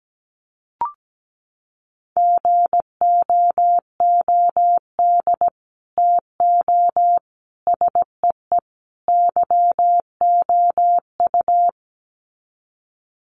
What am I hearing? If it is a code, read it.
Morse code, heard as a keyed tone, about 17 wpm, 700 Hz. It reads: GOOD TO SEE YOU
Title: Sets of 4 Words 17wpm